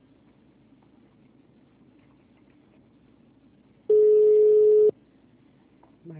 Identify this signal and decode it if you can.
Ringback tone (Europe, 425 Hz), with speech.